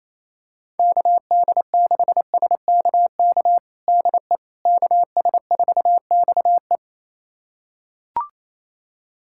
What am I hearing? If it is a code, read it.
Morse code, heard as a keyed tone, about 27 wpm, 700 Hz. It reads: KD6SKK DE KS4XE
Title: USA Callsign DE Callsign 28wpm